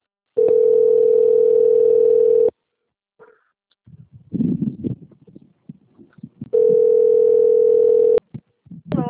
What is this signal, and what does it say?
Ringback tone (North America), 440 + 480 Hz.